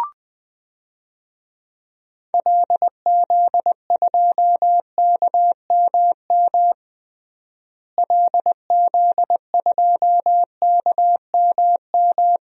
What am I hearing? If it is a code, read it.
Morse code, heard as a keyed tone, about 21 wpm, 700 Hz. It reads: LZ2KMM LZ2KMM
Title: DXCC-100 Call Signs - 6 Characters - Repeated 2x - 20wpm